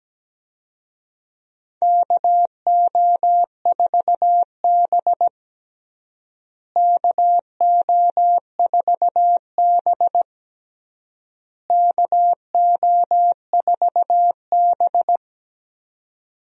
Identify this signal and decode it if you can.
Morse code, heard as a keyed tone, about 17 wpm, 700 Hz. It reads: KO4B KO4B KO4B
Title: USA Call Signs - 4 Characters - Repeated 3x - 17wpm